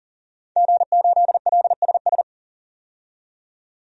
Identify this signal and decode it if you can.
Morse code, heard as a keyed tone, about 37 wpm, 700 Hz. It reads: G8LSS
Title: DXCC-100 Call Signs - 5 Characters - 40wpm